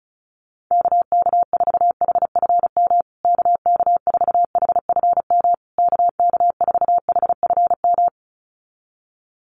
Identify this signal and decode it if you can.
Morse code, heard as a keyed tone, about 34 wpm, 700 Hz. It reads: KK4HFM KK4HFM KK4HFM
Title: USA Call Signs Repeated 3x 35wpm